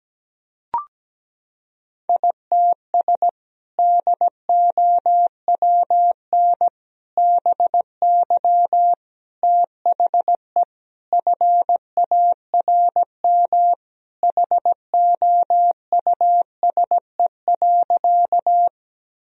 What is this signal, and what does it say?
Morse code, heard as a keyed tone, about 17 wpm, 700 Hz. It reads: ITS DOWN BY THE FARM HOUSE.
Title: Sentences from Top 1000 Words - Review Part 2 - 17wpm